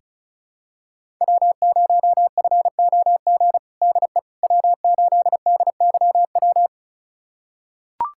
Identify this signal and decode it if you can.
Morse code, heard as a keyed tone, about 36 wpm, 700 Hz. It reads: W0FOG DE W8DYW
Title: USA Callsign DE Callsign - 5 Characters - 35wpm